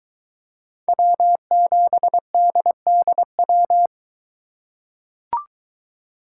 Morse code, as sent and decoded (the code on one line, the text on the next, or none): .-- --... -.. -.. .--
W7DDW